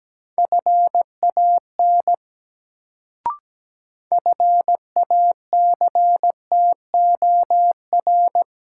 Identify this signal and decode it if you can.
Morse code, heard as a keyed tone, about 17 wpm, 700 Hz. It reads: FAN FACTOR
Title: Sentences from Top 1300 Words - New Word Drill Alt - 17wpm